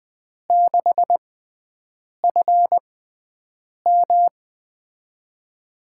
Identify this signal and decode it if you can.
Morse code, heard as a keyed tone, about 20 wpm, spaced as for 7 wpm, 700 Hz. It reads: 6FM
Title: USA Call Signs - 5 Characters - 6x Character Spacing - 20wpm